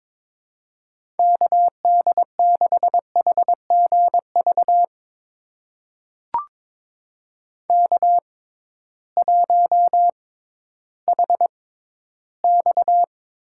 Morse code, heard as a keyed tone, about 23 wpm, spaced as for 13 wpm, 700 Hz. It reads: KD6HGV K1HX
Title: USA Call Signs - 6x Character Spacing - 22wpm